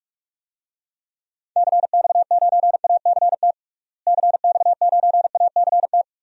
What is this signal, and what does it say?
Morse code, heard as a keyed tone, about 46 wpm, 700 Hz. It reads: CX9ACT CX9ACT
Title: DXCC-100 Call Signs - 6 Characters - Repeated 2x - 45wpm